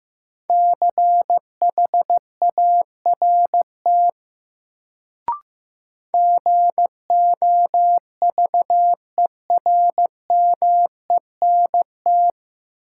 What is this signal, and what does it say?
Morse code, heard as a keyed tone, about 15 wpm, 700 Hz. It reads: CHART GOVERMENT
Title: Sentences from Top 1300 Words - New Word Drill Alt - 15wpm